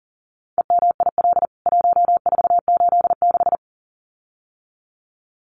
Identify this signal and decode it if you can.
Morse code, heard as a keyed tone, about 40 wpm, 700 Hz. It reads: EMIL 1486
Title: CWT Exchange 40wpm